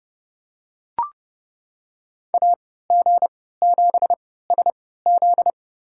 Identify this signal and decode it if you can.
Morse code, heard as a keyed tone, about 30 wpm, spaced as for 17 wpm, 700 Hz. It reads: AG7SZ
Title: USA Call Signs - 5 Characters - 3x Character Spacing - 30wpm